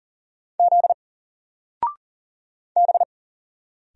Morse code, heard as a keyed tone, about 40 wpm, 700 Hz. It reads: Z B
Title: Single Letter-Number 40wpm